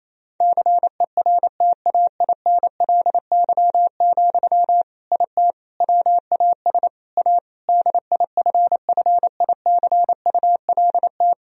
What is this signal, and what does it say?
Morse code, heard as a keyed tone, about 28 wpm, 700 Hz. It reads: CERTAINLY, IT WAS A DIFFICULT
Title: Sentences from Top 1000 Words - Review Part 2 - 28wpm